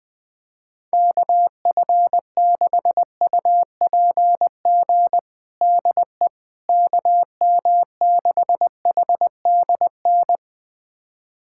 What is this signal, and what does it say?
Morse code, heard as a keyed tone, about 21 wpm, 700 Hz. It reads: KF6UPG DE KM6HDN
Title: USA Callsign DE Callsign - 6 Characters - 20wpm